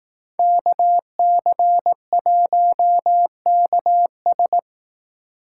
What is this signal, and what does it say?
Morse code, heard as a keyed tone, about 18 wpm, 700 Hz. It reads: KC1KS
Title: Speed Racing - USA Call Signs 15wpm